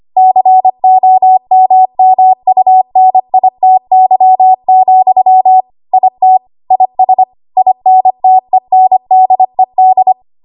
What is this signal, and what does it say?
Morse code, heard as a keyed tone, about 25 wpm, 750 Hz. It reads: COMMUNITY, IT IS INTENDED
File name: Morse / 25WPM